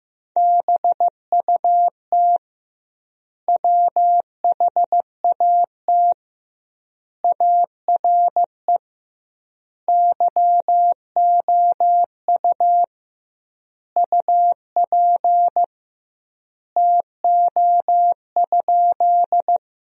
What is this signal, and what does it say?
Morse code, heard as a keyed tone, about 15 wpm, 700 Hz. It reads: BUT WHAT ARE YOU UP TO?
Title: 2x Word-Spacing - Sentences from Top 100 Words - 15wpm